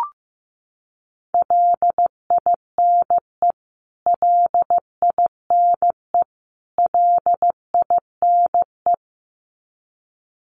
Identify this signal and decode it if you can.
Morse code, heard as a keyed tone, about 15 wpm, 700 Hz. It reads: LINE LINE LINE
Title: Top 200 Words Repeated 15wpm